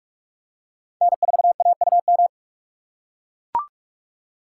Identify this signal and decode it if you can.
Morse code, heard as a keyed tone, about 44 wpm, 690 Hz. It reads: N4AUM
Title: USA Call Signs - 4x Character Spacing - 45wpm